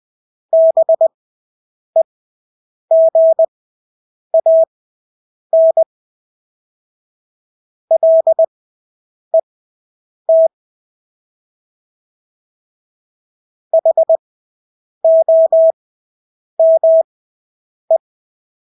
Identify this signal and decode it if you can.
Morse code, heard as a keyed tone, about 20 wpm, spaced as for 8 wpm, 650 Hz. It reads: BEGAN LET HOME
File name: Engelske ord 20/8 wpm